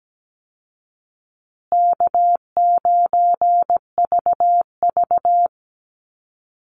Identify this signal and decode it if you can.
Morse code, heard as a keyed tone, about 17 wpm, 700 Hz. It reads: K9VV
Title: Easy Call Signs Repeated 3x - 17wpm